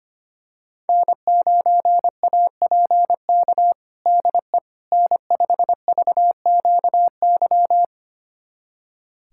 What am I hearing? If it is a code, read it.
Morse code, heard as a keyed tone, about 25 wpm, 700 Hz. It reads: N9APK DE N5VQY
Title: USA Callsign DE Callsign - 5 Characters - 25wpm